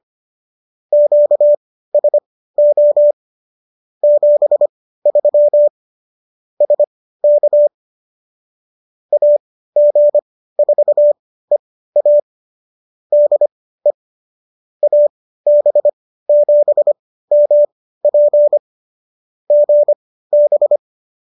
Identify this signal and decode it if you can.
Morse code, heard as a keyed tone, about 25 wpm, spaced as for 15 wpm, 590 Hz. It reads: QSO 73 SK AG4EA DE AB7MP GB